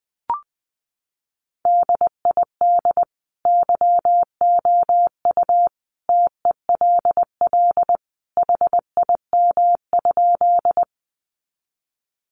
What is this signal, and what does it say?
Morse code, heard as a keyed tone, about 20 wpm, 700 Hz. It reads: DID YOU TELL HIM?
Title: Sentences from Top 200 Words 20wpm